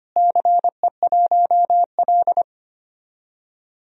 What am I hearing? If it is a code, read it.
Morse code, heard as a keyed tone, about 25 wpm, 700 Hz. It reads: CE1L
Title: DXCC-100 Call Signs - 4 Characters - Repeated 3x - 25wpm